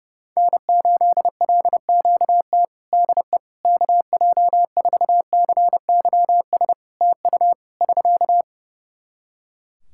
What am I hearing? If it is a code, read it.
Morse code, heard as a keyed tone, about 30 wpm, 700 Hz. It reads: N8LQT DE KJ4CYS TU <SK>
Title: K1USN SST QSO Practice 30wpm